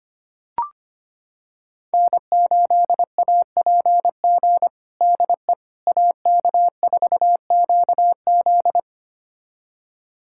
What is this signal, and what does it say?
Morse code, heard as a keyed tone, about 25 wpm, 700 Hz. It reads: N8APG DE AK4QZ
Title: USA Callsign DE Callsign - 5 Characters - 25wpm